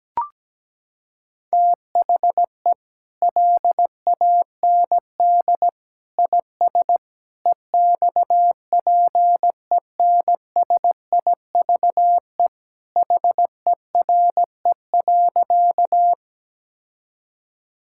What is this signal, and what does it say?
Morse code, heard as a keyed tone, about 17 wpm, 700 Hz. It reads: THE LAND IS EXPENSIVE HERE.
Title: Sentences from Top 1000 Words - Review Part 6 - 17wpm